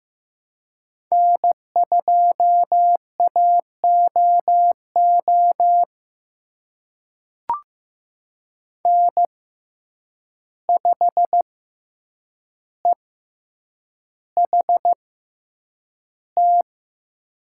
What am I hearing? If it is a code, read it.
Morse code, heard as a keyed tone, about 15 wpm, spaced as for 8 wpm, 700 Hz. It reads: N2AOO N5EHT
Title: USA Call Signs - 5 Characters - 6x Character Spacing - 15wpm